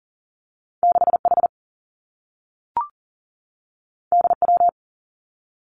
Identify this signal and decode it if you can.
Morse code, heard as a keyed tone, about 40 wpm, 700 Hz. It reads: 6H DW
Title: Two Challenging Letter-Number Combinations 40wpm